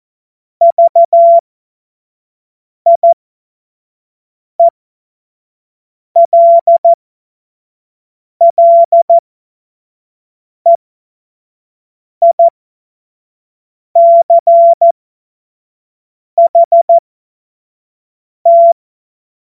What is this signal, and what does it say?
Morse code, heard as a keyed tone, about 14 wpm, spaced as for 5 wpm, 680 Hz. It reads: VIELLEICHT
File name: Morsekurs / Witze / Witz 11. Januar 2022, 25 F70